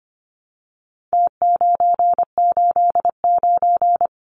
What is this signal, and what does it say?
Morse code, heard as a keyed tone, about 25 wpm, 700 Hz. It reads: T989
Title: Numbers - Contest Style - 4-Digit 25wpm